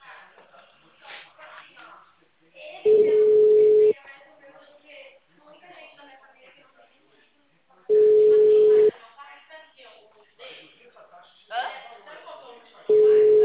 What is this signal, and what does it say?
Ringback tone (Europe, 425 Hz), with speech.
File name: out- -2012- - - .6